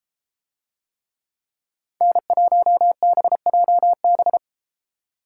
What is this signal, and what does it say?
Morse code, heard as a keyed tone, about 33 wpm, 700 Hz. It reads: N1BJB